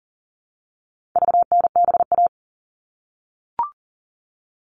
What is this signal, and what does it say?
Morse code, heard as a keyed tone, about 40 wpm, 700 Hz. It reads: VNBA
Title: All Cut Numbers - 4-Digit 40wpm